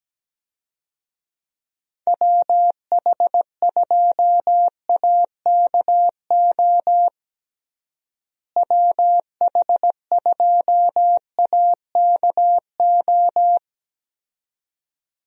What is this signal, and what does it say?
Morse code, heard as a keyed tone, about 17 wpm, 700 Hz. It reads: WH2AKO WH2AKO
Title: USA Call Signs - 6 Characters - Repeated 2x - 17wpm